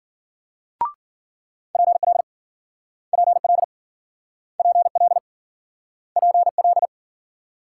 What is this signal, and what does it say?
Morse code, heard as a keyed tone, about 52 wpm, 700 Hz. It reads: PL PL PL PL